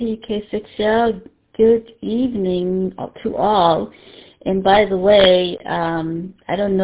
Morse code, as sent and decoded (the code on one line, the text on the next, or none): none